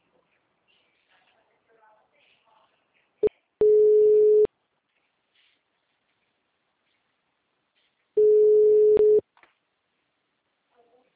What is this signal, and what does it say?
Ringback tone (Europe, 425 Hz), with speech.
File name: out- -2032- - - .8943